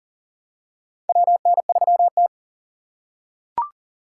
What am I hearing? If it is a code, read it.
Morse code, heard as a keyed tone, about 40 wpm, 700 Hz. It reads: WN3T
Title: USA Call Signs - 4 Characters - 4x Character Spacing - 40wpm